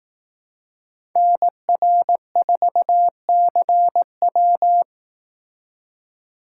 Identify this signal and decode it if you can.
Morse code, heard as a keyed tone, about 18 wpm, 700 Hz. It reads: NR4CW
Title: Speed Racing - USA 2x2 Call Signs 15wpm